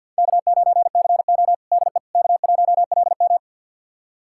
Morse code, heard as a keyed tone, about 51 wpm, 700 Hz. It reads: K9CO DE K1LM
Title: USA Callsign DE Callsign - 4 Characters - 50wpm